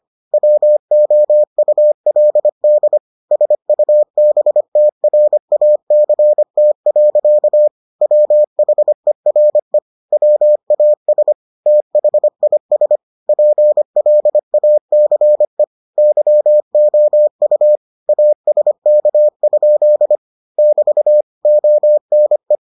Morse code, heard as a keyed tone, about 25 wpm, 600 Hz. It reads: WOULD SUBTRACT. WHERE WAS THIS PLACE YOU ASK? <BT> ONE